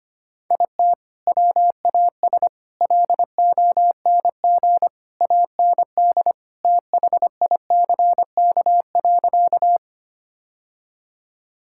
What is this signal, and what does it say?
Morse code, heard as a keyed tone, about 25 wpm, 700 Hz. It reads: IT WAS LONG AND THICK.
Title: Sentences from Top 1600 Words 25wpm